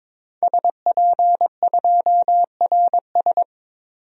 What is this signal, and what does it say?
Morse code, heard as a keyed tone, about 21 wpm, 700 Hz. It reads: SP2RS
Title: DXCC-100 Call Signs - 5 Characters - 22wpm